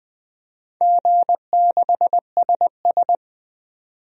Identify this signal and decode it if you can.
Morse code, heard as a keyed tone, about 21 wpm, 700 Hz. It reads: G6SS
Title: DXCC-100 Call Signs - 4 Characters - 20wpm